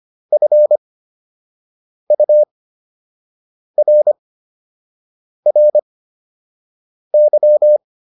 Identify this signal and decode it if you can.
Morse code, heard as a keyed tone, about 25 wpm, spaced as for 6 wpm, 600 Hz. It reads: FURRY